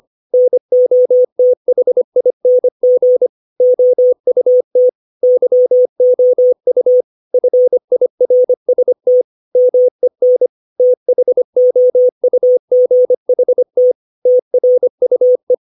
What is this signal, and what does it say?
Morse code, heard as a keyed tone, about 25 wpm, 500 Hz. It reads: NOTHING OUT YOU FIRST MEN THOUGHT TRUE
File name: 632 words / 632 Words 25WPM